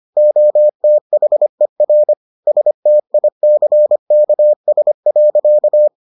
Morse code, heard as a keyed tone, about 25 wpm, 600 Hz. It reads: OTHER STICKS.